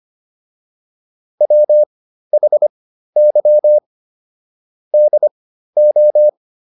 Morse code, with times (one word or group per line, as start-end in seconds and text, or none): 1.4-3.8 WHY
4.9-6.3 DO